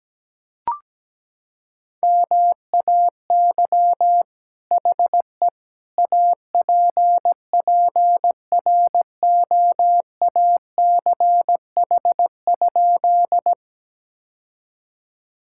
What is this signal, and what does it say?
Morse code, heard as a keyed tone, about 17 wpm, 700 Hz. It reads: MAY HE APPROACH?
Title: Sentences from Top 1000 Words - Review Part 6 - 17wpm